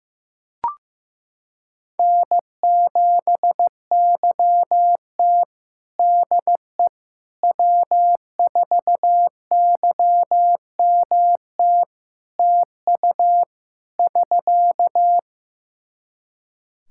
Morse code, heard as a keyed tone, about 16 wpm, 700 Hz. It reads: N7YT DE W4YMT TU <SK>
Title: K1USN SST QSO Practice 15wpm